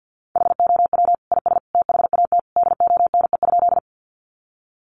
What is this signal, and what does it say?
Morse code, heard as a keyed tone, about 50 wpm, 700 Hz. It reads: HOW IS THAT DONE?
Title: Sentences from Top 500 Words 50wpm